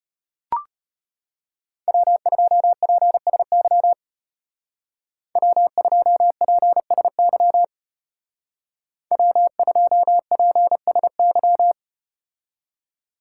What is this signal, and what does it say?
Morse code, heard as a keyed tone, about 32 wpm, 700 Hz. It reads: W2PSY W2PSY W2PSY